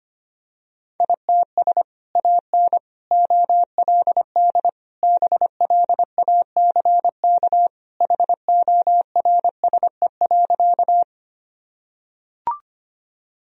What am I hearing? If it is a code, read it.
Morse code, heard as a keyed tone, about 25 wpm, 700 Hz. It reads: ITS AN OLD BLACK HORSE.
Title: Sentences from Top 1000 Words - Review Part 2 - 25wpm